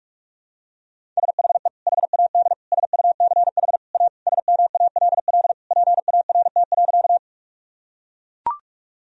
Morse code, heard as a keyed tone, about 41 wpm, 700 Hz. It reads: SHE HAD SUCH A SMALL PART.
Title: Sentences from Top 1000 Words - Review Part 1 - 45wpm